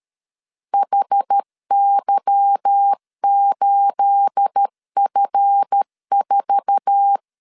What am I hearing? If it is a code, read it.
Morse code, heard as a keyed tone, about 13 wpm, 800 Hz. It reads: HY8F4
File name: LEZIONI / 9 28 12wpm 2351